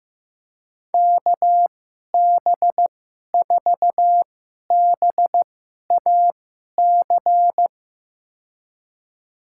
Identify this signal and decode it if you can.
Morse code, heard as a keyed tone, about 15 wpm, spaced as for 11 wpm, 700 Hz. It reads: KB4BAC
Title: USA Call Signs - 6 Characters - 2x Character Spacing - 15wpm